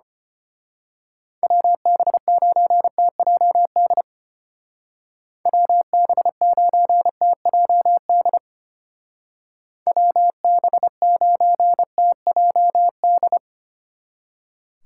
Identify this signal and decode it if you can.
Morse code, heard as a keyed tone, about 30 wpm, 700 Hz. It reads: WB9TJD WB9TJD WB9TJD
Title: Speed Racing - USA Call Signs 25wpm